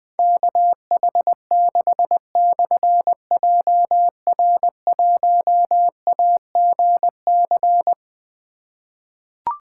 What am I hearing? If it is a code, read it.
Morse code, heard as a keyed tone, about 21 wpm, 700 Hz. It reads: KH6/JR1AGC
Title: Difficult Call Signs 20wpm